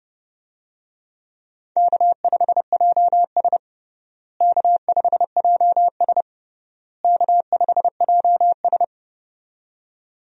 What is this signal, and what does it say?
Morse code, heard as a keyed tone, about 31 wpm, 700 Hz. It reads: K5JS K5JS K5JS
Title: USA Call Signs - 4 Characters - Repeated 3x - 30wpm